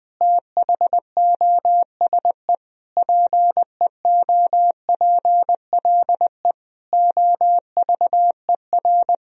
Morse code, heard as a keyed tone, about 21 wpm, 700 Hz. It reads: THOSE PEOPLE OVER
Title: Sentences from Top 1000 Words - Review Part 1 - 20wpm